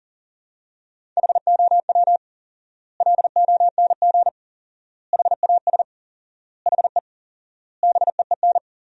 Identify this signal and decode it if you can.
Morse code, heard as a keyed tone, about 40 wpm, 700 Hz. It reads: HOW LONG HAS HE BEEN
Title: 4x Word-Spacing - Sentences from Top 100 Words - 40wpm